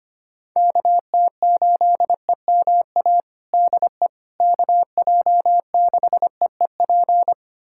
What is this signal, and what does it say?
Morse code, heard as a keyed tone, about 25 wpm, 700 Hz. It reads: KT8EMA DE KJ6EEP
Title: USA Callsign DE Callsign 25wpm